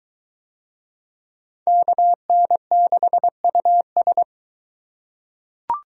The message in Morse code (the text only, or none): KN6US